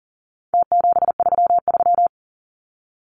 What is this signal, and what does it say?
Morse code, heard as a keyed tone, about 40 wpm, 700 Hz. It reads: T733
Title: Numbers - Contest Style - 4-Digit 40wpm